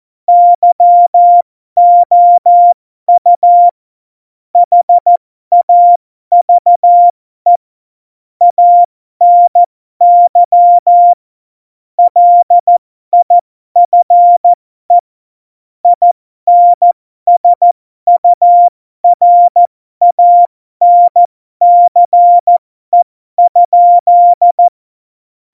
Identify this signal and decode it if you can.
Morse code, heard as a keyed tone, about 14 wpm, spaced as for 12 wpm, 700 Hz. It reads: YOU HAVE ANY LIFE INSURANCE?
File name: Morsekurs / Witze